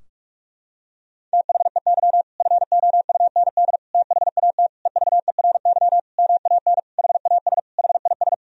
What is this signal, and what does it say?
Morse code, heard as a keyed tone, about 46 wpm, 700 Hz. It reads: THEY FOUND THAT EVERY MAN HAS HIS
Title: Sentences from Top 1000 Words - Review Part 4 - 45wpm